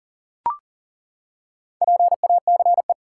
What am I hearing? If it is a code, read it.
Morse code, heard as a keyed tone, about 39 wpm, 700 Hz. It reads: PACE
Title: Four Letter Words - 40wpm